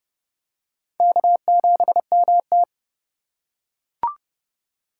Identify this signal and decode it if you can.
Morse code, heard as a keyed tone, about 27 wpm, 700 Hz. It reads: K7MT